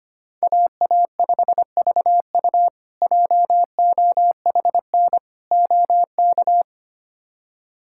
Morse code, heard as a keyed tone, about 25 wpm, 700 Hz. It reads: AA5VU JOHN OK
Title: K1USN SST QSO Practice - 25wpm